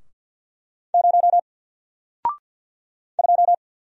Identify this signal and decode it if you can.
Morse code, heard as a keyed tone, about 48 wpm, 700 Hz. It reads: 0 2
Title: Single Letter-Number Alt 50wpm